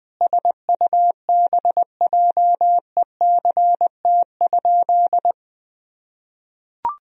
Morse code, heard as a keyed tone, about 19 wpm, 700 Hz. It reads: SUBJECT?
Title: Sentences from Top 1000 Words - Review Part 4 - 20wpm